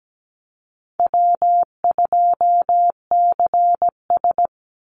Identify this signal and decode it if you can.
Morse code, heard as a keyed tone, about 17 wpm, 700 Hz. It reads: W2CS
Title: Easy Call Signs - 17wpm